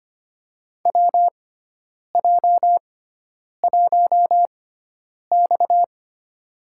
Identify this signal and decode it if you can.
Morse code, heard as a keyed tone, about 25 wpm, spaced as for 9 wpm, 700 Hz. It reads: WJ1X
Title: USA Call Signs - 6x Character Spacing - 25wpm